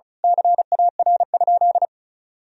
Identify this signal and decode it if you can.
Morse code, heard as a keyed tone, about 34 wpm, 700 Hz. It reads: CAR?
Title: Sentences from Top 1000 Words - Review Part 1 - 35wpm